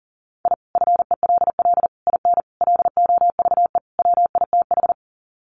Morse code, heard as a keyed tone, about 40 wpm, 700 Hz. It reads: I FELL IN LOVE WITH
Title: Sets of 5 Words Encore 40wpm